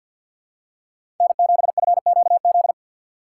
Morse code, heard as a keyed tone, about 51 wpm, 700 Hz. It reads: N7FQZ
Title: USA Call Signs - 5 Characters - 8x Character Spacing - 50wpm